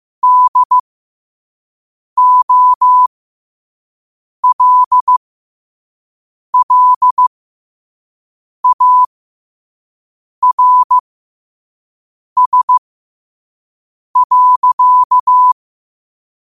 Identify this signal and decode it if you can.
Morse code, heard as a keyed tone, about 15 wpm, spaced as for 5 wpm, 1000 Hz. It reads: DOLLARS.
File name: qotdfiles / 5wpm